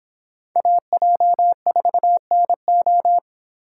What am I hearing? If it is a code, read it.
Morse code, heard as a keyed tone, about 26 wpm, 700 Hz. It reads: AJ4NO